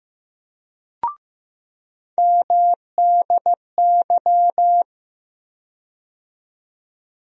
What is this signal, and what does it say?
Morse code, heard as a keyed tone, about 15 wpm, 700 Hz. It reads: MDY
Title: USA Call Sign Postfixes - 3 Letters 15wpm